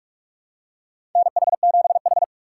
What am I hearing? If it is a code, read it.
Morse code, heard as a keyed tone, about 44 wpm, 700 Hz. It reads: NH7H
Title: USA Call Signs - 6x Character Spacing - 45wpm